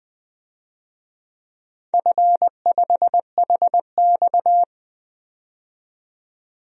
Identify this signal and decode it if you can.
Morse code, heard as a keyed tone, about 20 wpm, 700 Hz. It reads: F5HX
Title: DXCC-100 Call Signs - 4 Characters - 20wpm